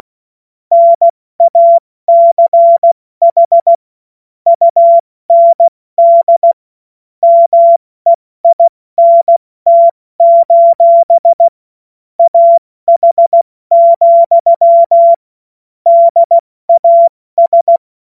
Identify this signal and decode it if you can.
Morse code, heard as a keyed tone, about 16 wpm, 680 Hz. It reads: NACH UND MEINT: AH, DAS
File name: Morsekurs / Witze / Witz 14. Juni 2022 70 F80